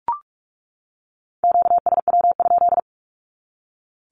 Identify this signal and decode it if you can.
Morse code, heard as a keyed tone, about 44 wpm, 700 Hz. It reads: QSW?
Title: Q-Codes 45wpm